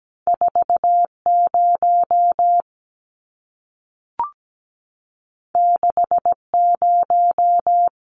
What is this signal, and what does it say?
Morse code, heard as a keyed tone, about 17 wpm, 700 Hz. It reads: 40 60
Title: Numbers - 2-Digit 17wpm